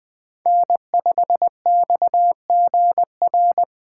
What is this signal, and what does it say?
Morse code, heard as a keyed tone, about 21 wpm, 700 Hz. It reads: N5XGR
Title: USA Call Signs - 5 Characters - 2x Character Spacing - 20wpm